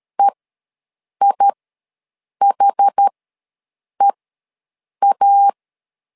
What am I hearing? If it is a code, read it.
Morse code, heard as a keyed tone, about 13 wpm, spaced as for 7 wpm, 800 Hz. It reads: EIHEA
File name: LEZIONI / 2 UHDN / 2 UH ws 12wpm 1730